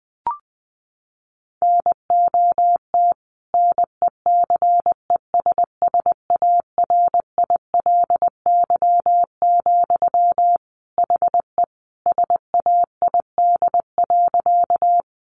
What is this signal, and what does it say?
Morse code, heard as a keyed tone, about 20 wpm, 700 Hz. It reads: NOT NECESSARILY, HE SAID.
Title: Sentences from Top 1400 Words 20wpm